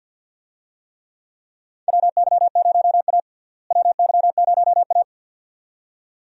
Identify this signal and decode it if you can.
Morse code, heard as a keyed tone, about 52 wpm, 700 Hz. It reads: WY0A WY0A
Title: USA Call Signs - 4 Characters - Repeated 2x - 50wpm